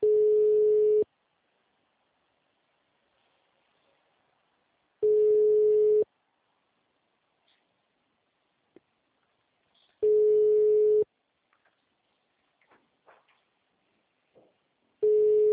Ringback tone (Europe, 425 Hz).